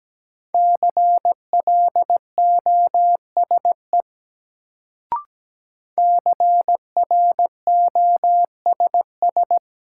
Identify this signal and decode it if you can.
Morse code, heard as a keyed tone, about 17 wpm, 700 Hz. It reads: CLOSE CROSS
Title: Sentences from Top 300 Words - New Word Drill - 17wpm